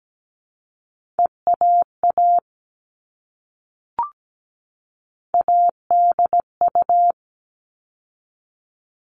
Morse code, heard as a keyed tone, about 17 wpm, 700 Hz. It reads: EAA ADU
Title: All Cut Numbers - 3-Digit 17wpm